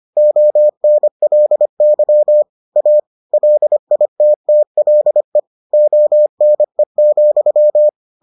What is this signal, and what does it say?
Morse code, heard as a keyed tone, about 25 wpm, 600 Hz. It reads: ONLY A LITTLE ONE,